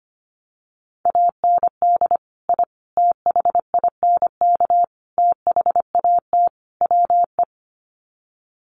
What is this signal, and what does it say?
Morse code, heard as a keyed tone, about 25 wpm, 700 Hz. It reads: AND I THINK THAT WE
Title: Sets of 5 Words Encore2 25wpm